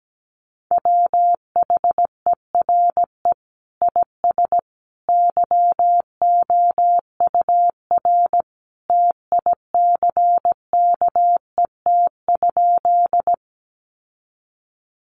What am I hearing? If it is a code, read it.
Morse code, heard as a keyed tone, about 17 wpm, 700 Hz. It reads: WHERE IS YOUR TICKET?
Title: Sentences from Top 1400 Words 17wpm